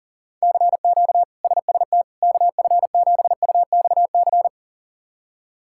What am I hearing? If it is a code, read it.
Morse code, heard as a keyed tone, about 37 wpm, 700 Hz. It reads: CQ SST KF7UXC
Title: K1USN SST QSO Practice - 40wpm